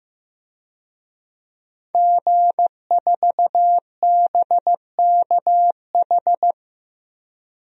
Morse code, heard as a keyed tone, about 15 wpm, 700 Hz. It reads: G4BKH
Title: DXCC-100 Call Signs - 5 Characters - Repeated 3x - 15wpm